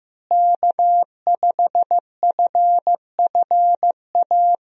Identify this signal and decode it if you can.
Morse code, heard as a keyed tone, about 16 wpm, 700 Hz. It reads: K5FFA
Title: USA Call Signs - 5 Characters - Repeated 3x - 15wpm